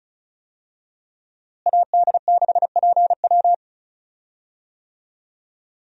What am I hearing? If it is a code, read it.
Morse code, heard as a keyed tone, about 36 wpm, 700 Hz. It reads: AD6PW
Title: USA Call Signs - 5 Characters - Repeated 2x - 35wpm